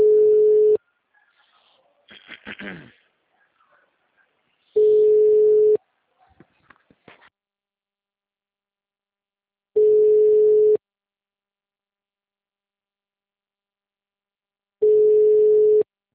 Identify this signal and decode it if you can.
Ringback tone (Europe, 425 Hz).